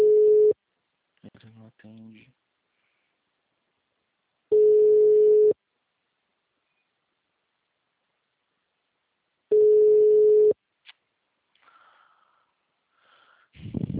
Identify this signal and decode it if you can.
Ringback tone (Europe, 425 Hz), with speech.
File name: out- -2040- - - .40963